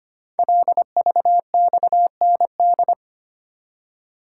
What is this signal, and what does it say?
Morse code, heard as a keyed tone, about 25 wpm, 700 Hz. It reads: LVXND
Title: Code Groups - 5 Letters - 25wpm